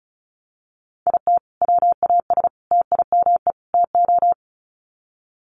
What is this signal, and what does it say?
Morse code, heard as a keyed tone, about 34 wpm, 700 Hz. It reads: IT WAS TIME TO
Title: Sets of 4 Words 35wpm